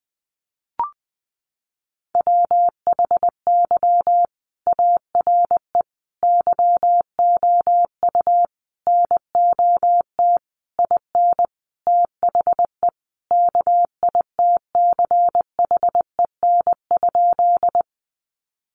Morse code, heard as a keyed tone, about 20 wpm, 700 Hz. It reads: WHY ARE YOU NOT IN THE KITCHEN?
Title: Sentences from Top 1400 Words 20wpm